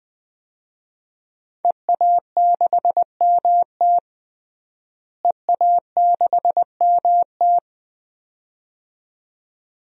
Morse code, heard as a keyed tone, about 20 wpm, 700 Hz. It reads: EA6MT EA6MT
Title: DXCC-100 Call Signs - 5 Characters - Repeated 2x - 20wpm